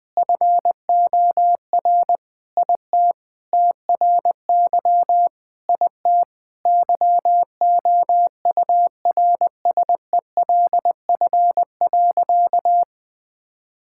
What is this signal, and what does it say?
Morse code, heard as a keyed tone, about 20 wpm, 700 Hz. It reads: FOR IT TRY IT YOURSELF.
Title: Sentences with 7-Word Phrases - 20wpm